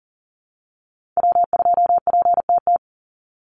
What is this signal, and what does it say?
Morse code, heard as a keyed tone, about 40 wpm, 700 Hz. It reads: W2PTT
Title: Call Signs Repeated 2x - 40wpm